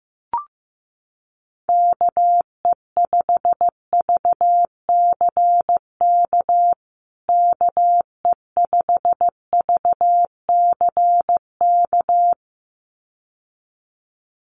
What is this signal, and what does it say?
Morse code, heard as a keyed tone, about 15 wpm, 700 Hz. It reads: KE5VCK KE5VCK
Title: USA Call Signs Repeated 2x 15wpm